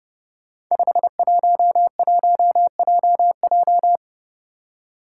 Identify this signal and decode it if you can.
Morse code, heard as a keyed tone, about 30 wpm, 700 Hz. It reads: 511JJ
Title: Five Challenging Letter-Number Combinations 30wpm